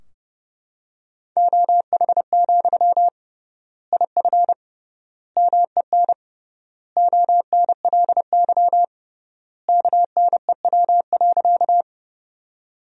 Morse code, heard as a keyed tone, about 29 wpm, 700 Hz. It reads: OH, IF MEN ONLY KNEW.
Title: 3x Word-Spacing - Sentences from Top 500 Words - 30wpm